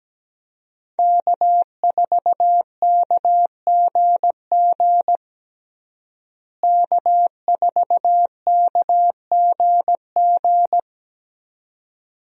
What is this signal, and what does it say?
Morse code, heard as a keyed tone, about 17 wpm, 700 Hz. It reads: K4KGG K4KGG
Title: USA Call Signs - 5 Characters - Repeated 2x - 17wpm